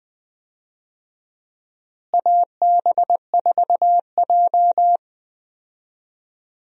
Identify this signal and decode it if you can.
Morse code, heard as a keyed tone, about 20 wpm, 700 Hz. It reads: AB4J
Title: USA Call Signs - 4 Characters - Repeated 2x - 20wpm